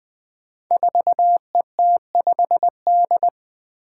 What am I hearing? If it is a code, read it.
Morse code, heard as a keyed tone, about 20 wpm, 700 Hz. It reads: 4ET5D
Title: Five Challenging Letter-Number Combinations 20wpm